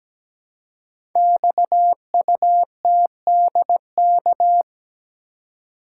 Morse code, heard as a keyed tone, about 17 wpm, 700 Hz. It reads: XUTDK
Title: Code Groups - 5 Letters - 17wpm